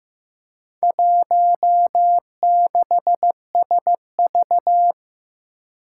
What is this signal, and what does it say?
Morse code, heard as a keyed tone, about 15 wpm, 700 Hz. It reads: 16SV
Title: Four Challenging Letter-Number Combinations 15wpm